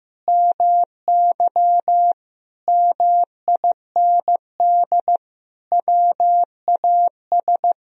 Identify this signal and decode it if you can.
Morse code, heard as a keyed tone, about 16 wpm, 700 Hz. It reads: MY MIND WAS
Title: Sentences from Top 1000 Words - Review Part 2 - 15wpm